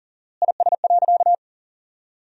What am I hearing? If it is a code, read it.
Morse code, heard as a keyed tone, about 40 wpm, 700 Hz. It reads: IS.
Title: 4x Word-Spacing - Sentences from Top 300 Words - 40wpm